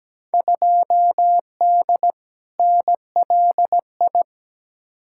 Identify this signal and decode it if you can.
Morse code, heard as a keyed tone, about 17 wpm, 700 Hz. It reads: 2D NLI
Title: Field Day Exchanges - 17wpm